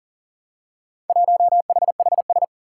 Morse code, heard as a keyed tone, about 38 wpm, 700 Hz. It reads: 1HHS
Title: Four Challenging Letter-Number Combinations 40wpm